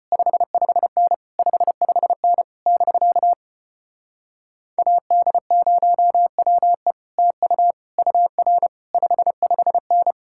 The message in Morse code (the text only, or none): 55N 55N <BK> AD0WE TU UR 55N